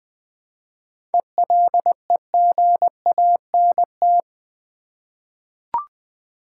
Morse code, heard as a keyed tone, about 20 wpm, 700 Hz. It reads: ELEGANT
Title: 7 Letter Words - First 3 Letters Given - 20wpm